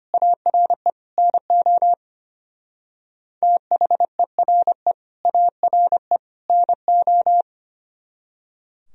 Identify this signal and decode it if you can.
Morse code, heard as a keyed tone, about 30 wpm, 700 Hz. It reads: ARE NO THERE ARE NO
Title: Speed-Racing - Sets of 3 Words 25wpm